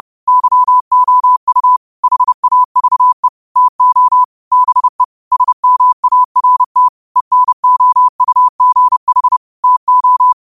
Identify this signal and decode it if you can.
Morse code, heard as a keyed tone, about 30 wpm, 1000 Hz. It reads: YOU HAVE TO BE SMART ENOUGH TO
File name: qotdfiles / 30wpm